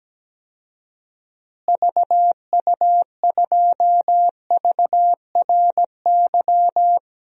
Morse code, heard as a keyed tone, about 17 wpm, 700 Hz. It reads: VU2VRY
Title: DXCC-100 Call Signs - 6 Characters - Repeated 2x - 17wpm